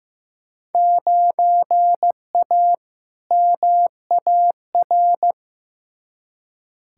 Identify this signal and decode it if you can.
Morse code, heard as a keyed tone, about 15 wpm, 700 Hz. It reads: 9A MAR
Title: Field Day Exchanges Encore - 15wpm